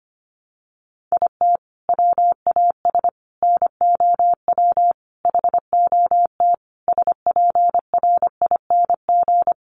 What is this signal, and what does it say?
Morse code, heard as a keyed tone, about 25 wpm, 700 Hz. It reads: IT WAS NOW HOT SPRING
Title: Sentences from Top 1200 Words 25wpm